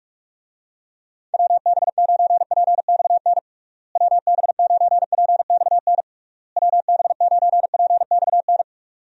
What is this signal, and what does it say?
Morse code, heard as a keyed tone, about 46 wpm, 700 Hz. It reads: WB9PXN WB9PXN WB9PXN
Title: USA Call Signs - 6 Characters - Repeated 3x - 45wpm